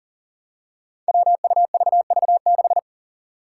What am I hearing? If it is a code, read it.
Morse code, heard as a keyed tone, about 40 wpm, 700 Hz. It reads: WUVV6
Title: Five Challenging Letter-Number Combinations 40wpm